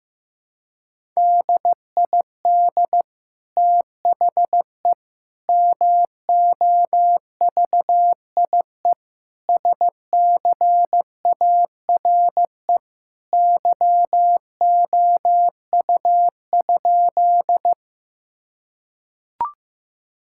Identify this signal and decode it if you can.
Morse code, heard as a keyed tone, about 15 wpm, 700 Hz. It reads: DID THE MOVIE SCARE YOU?
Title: Sentences from Top 1000 Words - Review Part 5 - 15wpm